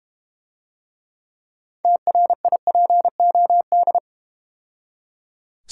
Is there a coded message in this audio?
Yes: Morse code from 1.9 to 4.0 s, heard as a keyed tone, about 31 wpm, 700 Hz. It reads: TRIPOD